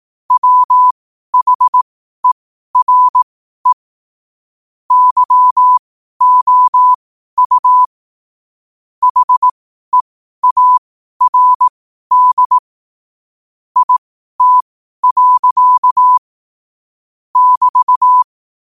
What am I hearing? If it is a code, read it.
Morse code, heard as a keyed tone, about 18 wpm, spaced as for 12 wpm, 1000 Hz. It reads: WHERE YOU HEARD IT. <BT>
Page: 12-wpm-morse-code-quotes